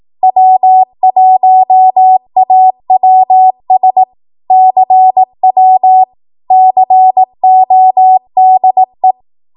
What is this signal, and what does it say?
Morse code, heard as a keyed tone, about 18 wpm, 750 Hz. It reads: W1AWS CW CODE
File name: Archive / 18 WPM / 18WPM